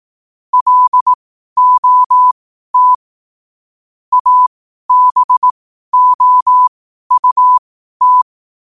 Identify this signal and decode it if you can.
Morse code, heard as a keyed tone, about 18 wpm, spaced as for 12 wpm, 1000 Hz. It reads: LOT ABOUT
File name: qotdfiles / 12wpm